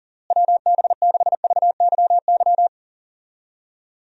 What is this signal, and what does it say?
Morse code, heard as a keyed tone, about 40 wpm, 700 Hz. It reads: WB6VYY
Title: USA Call Signs - 6 Characters - Repeated 2x - 40wpm